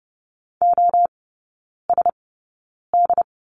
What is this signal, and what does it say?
Morse code, heard as a keyed tone, about 30 wpm, spaced as for 9 wpm, 700 Hz. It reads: OSD
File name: course / 30:10wpm / 087 Lesson 15e - Number 5 - Three Characters 30wpm